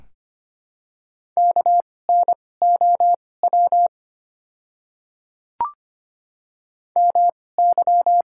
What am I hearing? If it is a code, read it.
Morse code, heard as a keyed tone, about 25 wpm, spaced as for 18 wpm, 700 Hz. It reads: KNOW MY
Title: 2x Character Spacing - Top 100 Words - 25wpm